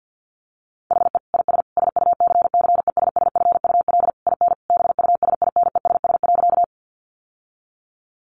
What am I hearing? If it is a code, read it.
Morse code, heard as a keyed tone, about 50 wpm, 700 Hz. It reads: HE IS SUCCESSFUL IN BUSINESS.
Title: Sentences from Top 900 Words 50wpm